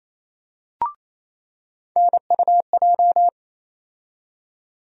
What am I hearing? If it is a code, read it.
Morse code, heard as a keyed tone, about 27 wpm, 700 Hz. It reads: NUJ
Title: USA Call Sign Postfixes - 3 Letters 28wpm